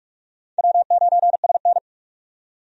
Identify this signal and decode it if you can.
Morse code, heard as a keyed tone, about 46 wpm, 700 Hz. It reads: W9SN
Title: USA Call Signs - 4 Characters - 45wpm